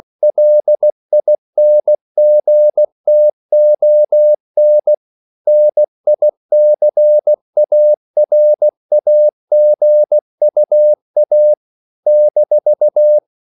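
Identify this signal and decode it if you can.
Morse code, heard as a keyed tone, about 16 wpm, 600 Hz. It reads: LINGTON NICARAGUA -
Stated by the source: CW practice of national capitals at 16 words per minute, from ON6ZQ.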